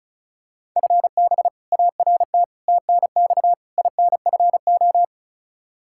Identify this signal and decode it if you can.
Morse code, heard as a keyed tone, about 34 wpm, 700 Hz. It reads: FB ART TNX INFO
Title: K1USN SST QSO Practice 35wpm